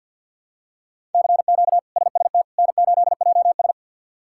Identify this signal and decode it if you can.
Morse code, heard as a keyed tone, about 48 wpm, 700 Hz. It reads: CQ SST N8JS
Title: K1USN SST QSO Practice - 50wpm